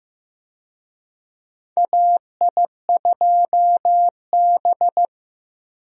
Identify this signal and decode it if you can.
Morse code, heard as a keyed tone, about 15 wpm, 700 Hz. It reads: AI2B
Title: USA Call Signs - 4 Characters - Repeated 2x - 15wpm